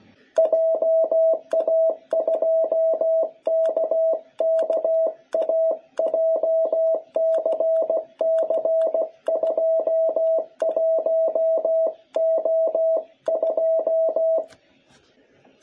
Morse code, heard as a keyed tone, about 16 wpm, 660 Hz. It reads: JA2KKA JCC21O2
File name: jh8cbh / 2021cw